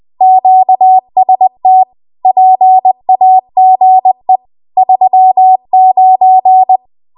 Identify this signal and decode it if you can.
Morse code, heard as a keyed tone, about 20 wpm, 750 Hz. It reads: QST PAGE 39
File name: Archive / 20 WPM / 20WPM